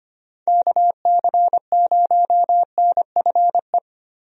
Morse code, heard as a keyed tone, about 25 wpm, 700 Hz. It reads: KC0NFE
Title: USA Call Signs - 6 Characters - 25wpm